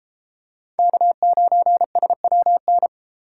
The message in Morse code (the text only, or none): K9SWN